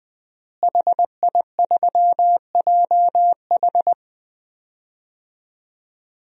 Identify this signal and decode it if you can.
Morse code, heard as a keyed tone, about 21 wpm, 700 Hz. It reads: HI3JH
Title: DXCC-100 Call Signs - 5 Characters - Repeated 2x - 20wpm